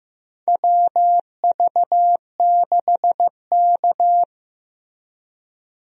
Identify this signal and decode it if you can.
Morse code, heard as a keyed tone, about 15 wpm, 700 Hz. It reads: WV6K
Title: USA Call Signs - 4 Characters - Repeated 2x - 15wpm